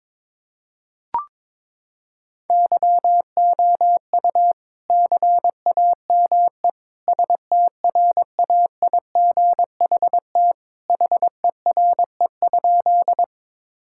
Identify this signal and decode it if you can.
Morse code, heard as a keyed tone, about 22 wpm, 700 Hz. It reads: YOU CAME STRAIGHT HERE?
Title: Sentences from Top 1000 Words - Review Part 6 - 22wpm